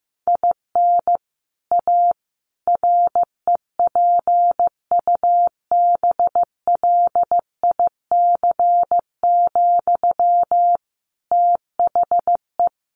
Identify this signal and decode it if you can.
Morse code, heard as a keyed tone, about 15 wpm, 700 Hz. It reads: IN A REPUBLIC, THE
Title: Sentences from Top 1500 Words 15wpm